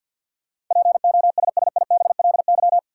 Morse code, heard as a keyed tone, about 53 wpm, 700 Hz. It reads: POSSIBLY
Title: Sentences from Top 1300 Words - New Word Drill - 50wpm